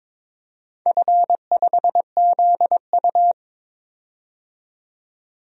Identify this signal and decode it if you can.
Morse code, heard as a keyed tone, about 22 wpm, 700 Hz. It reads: F5ZU
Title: DXCC-100 Call Signs - 4 Characters - 22wpm